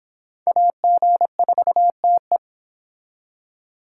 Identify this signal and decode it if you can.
Morse code, heard as a keyed tone, about 28 wpm, 700 Hz. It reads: AG4TE